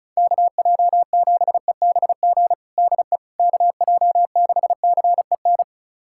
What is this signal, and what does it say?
Morse code, heard as a keyed tone, about 36 wpm, 700 Hz. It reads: KJ7EBG DE KJ6CEN
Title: USA Callsign DE Callsign - 6 Characters - 35wpm